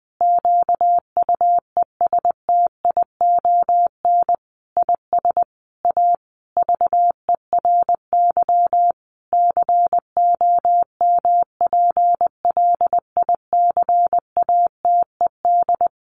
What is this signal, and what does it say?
Morse code, heard as a keyed tone, about 20 wpm, 700 Hz. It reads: QUESTION IS A VERY COMPLICATED
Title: Sentences from Top 1200 Words 20wpm